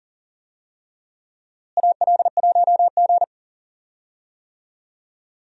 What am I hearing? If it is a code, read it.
Morse code, heard as a keyed tone, about 40 wpm, 700 Hz. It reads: AL1G
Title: USA Call Signs - 4 Characters - Repeated 2x - 40wpm